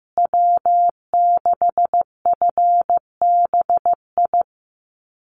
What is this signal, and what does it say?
Morse code, heard as a keyed tone, about 15 wpm, 700 Hz. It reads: W6FBI
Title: USA Call Signs Repeated 2x 15wpm